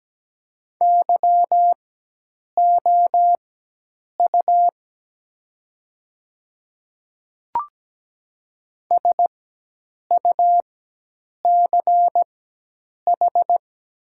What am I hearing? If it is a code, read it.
Morse code, heard as a keyed tone, about 17 wpm, spaced as for 8 wpm, 700 Hz. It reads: YOU SUCH
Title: 4x Character Spacing - Top 300 Words - 17wpm